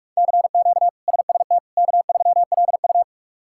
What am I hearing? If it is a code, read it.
Morse code, heard as a keyed tone, about 41 wpm, 700 Hz. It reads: CQ SST K3LU
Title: K1USN SST QSO Practice - 45wpm